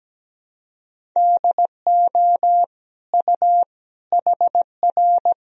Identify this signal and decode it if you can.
Morse code, heard as a keyed tone, about 17 wpm, 700 Hz. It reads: DO U HR
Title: Sentences from Top 300 Words and Abbreviations 17wpm